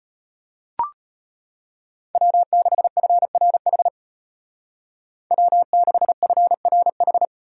Morse code, heard as a keyed tone, about 33 wpm, 700 Hz. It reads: W6FRH W6FRH